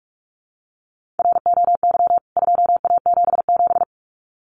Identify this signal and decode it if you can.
Morse code, heard as a keyed tone, about 44 wpm, 700 Hz. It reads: ROY 2A77
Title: CWT Exchange 45wpm